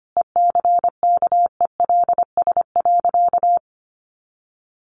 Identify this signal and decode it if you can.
Morse code, heard as a keyed tone, about 25 wpm, 700 Hz. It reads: ECKELS.
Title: A Sound Of Thunder 25wpm